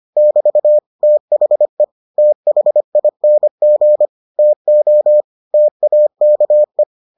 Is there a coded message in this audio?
Yes: Morse code from 0.2 to 6.8 s, heard as a keyed tone, about 25 wpm, 600 Hz. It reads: <BT> THE THING TO TAKE